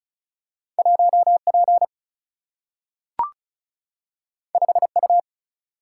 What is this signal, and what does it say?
Morse code, heard as a keyed tone, about 36 wpm, 700 Hz. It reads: 1P 5U
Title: Single-Letter Single-Number 35wpm